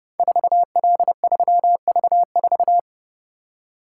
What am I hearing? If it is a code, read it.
Morse code, heard as a keyed tone, about 32 wpm, 700 Hz. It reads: 4L3V4
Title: Code Groups - 5 Letter:Numbers - 30wpm